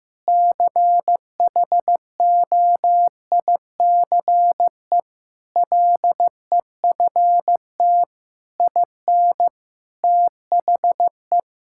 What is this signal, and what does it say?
Morse code, heard as a keyed tone, about 16 wpm, 700 Hz. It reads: CHOICE LEFT IN THE
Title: Sentences from Top 1000 Words - Review Part 6 - 15wpm